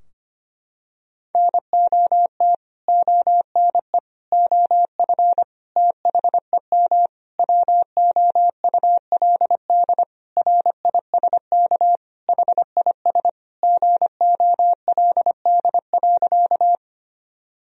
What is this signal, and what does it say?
Morse code, heard as a keyed tone, about 25 wpm, 700 Hz. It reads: NOT ONE OF THEM WOULD RISK HIS GOLD.
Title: Sentences from Top 1000 Words - Review Part 4 - 25wpm